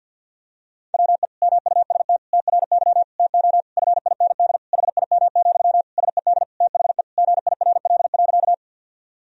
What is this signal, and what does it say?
Morse code, heard as a keyed tone, about 52 wpm, 700 Hz. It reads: WE MUST TRY TO FIND HIM, SED THE GIRL.
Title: Sentences from Top 400 Words and Abbreviations 50wpm